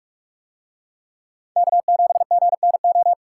Morse code, heard as a keyed tone, about 44 wpm, 700 Hz. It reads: K7GNO